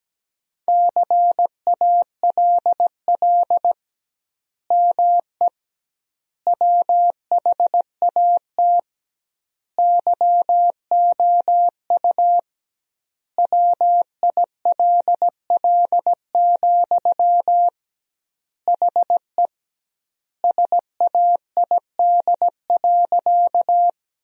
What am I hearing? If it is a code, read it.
Morse code, heard as a keyed tone, about 17 wpm, 700 Hz. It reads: CALL ME WHAT YOU WILL, HE SAID.
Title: 2x Word-Spacing - Sentences from Top 200 Words - 17wpm